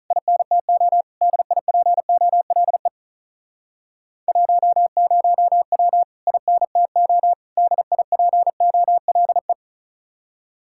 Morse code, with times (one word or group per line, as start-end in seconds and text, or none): 0.1-1.0 INTO
1.2-2.9 DIPOLE
4.3-6.0 10W
6.3-7.3 INTO
7.6-9.5 DIPOLE